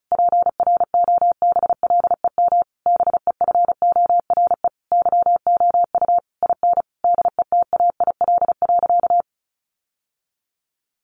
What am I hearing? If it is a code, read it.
Morse code, heard as a keyed tone, about 34 wpm, 700 Hz. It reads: PROBLEM BEFORE YOU IN DETAIL.
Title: Sentences from Top 1500 Words 35wpm